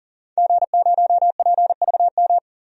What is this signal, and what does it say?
Morse code, heard as a keyed tone, about 40 wpm, 700 Hz. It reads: G0PVM
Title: DXCC-100 Call Signs - 5 Characters - 40wpm